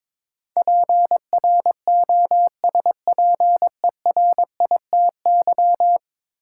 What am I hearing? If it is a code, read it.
Morse code, heard as a keyed tone, about 21 wpm, 700 Hz. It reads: PROSPERITY
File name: practice / 22wpm / 10 Letter Words - First 3 Letters Given - 22wpm